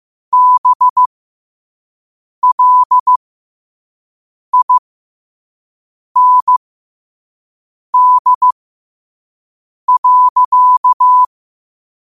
Morse code, heard as a keyed tone, about 15 wpm, spaced as for 5 wpm, 1000 Hz. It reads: BLIND.
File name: qotdfiles / 5wpm